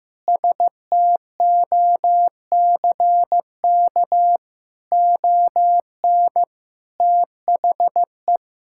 Morse code, heard as a keyed tone, about 15 wpm, 700 Hz. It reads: STOCK ON THE
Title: Sentences from Top 1000 Words - Review Part 5 - 15wpm